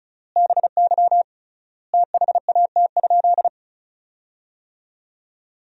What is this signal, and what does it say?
Morse code, heard as a keyed tone, about 34 wpm, 700 Hz. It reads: BY THAT?
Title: 3x Word-Spacing - Sentences from Top 200 Words - 35wpm